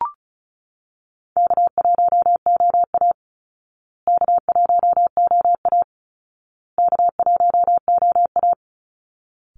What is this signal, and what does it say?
Morse code, heard as a keyed tone, about 34 wpm, 700 Hz. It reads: K1OA K1OA K1OA
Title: Easy Call Signs Repeated 3x - 35wpm